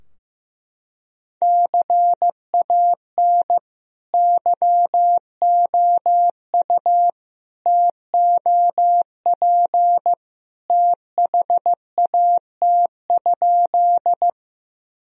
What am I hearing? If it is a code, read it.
Morse code, heard as a keyed tone, about 15 wpm, 700 Hz. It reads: CAN YOU TOP THAT?
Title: Sentences from Top 1000 Words - Review Part 2 - 15wpm